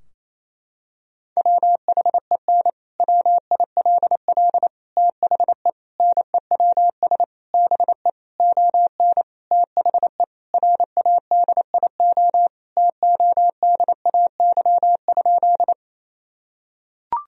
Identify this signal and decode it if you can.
Morse code, heard as a keyed tone, about 28 wpm, 700 Hz. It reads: WHEN WILL THE NEWS BE ON THE RADIO TODAY?
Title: Sentences from Top 1000 Words - Review Part 3 - 28wpm